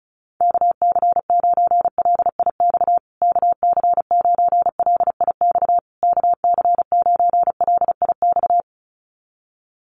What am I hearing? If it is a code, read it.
Morse code, heard as a keyed tone, about 34 wpm, 700 Hz. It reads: KC9LIX KC9LIX KC9LIX
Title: USA Call Signs Repeated 3x 35wpm